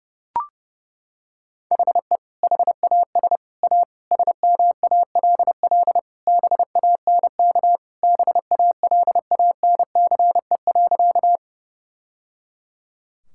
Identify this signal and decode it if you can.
Morse code, heard as a keyed tone, about 30 wpm, 700 Hz. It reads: HE HAS A SMALL BANK BALANCE.
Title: Sentences from Top 1000 Words - Review Part 6 - 30wpm